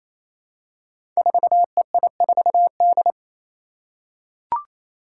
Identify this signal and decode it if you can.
Morse code, heard as a keyed tone, about 27 wpm, 700 Hz. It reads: 4EI4D
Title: Five Challenging Letter-Number Combinations 28wpm